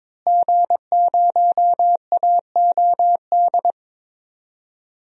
Morse code, heard as a keyed tone, about 21 wpm, 700 Hz. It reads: G0AOD
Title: DXCC-100 Call Signs - 5 Characters - Repeated 2x - 22wpm